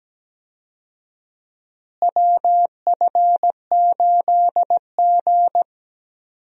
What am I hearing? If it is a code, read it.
Morse code, heard as a keyed tone, about 17 wpm, 700 Hz. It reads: WF8G
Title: USA Call Signs - 4 Characters - Repeated 3x - 17wpm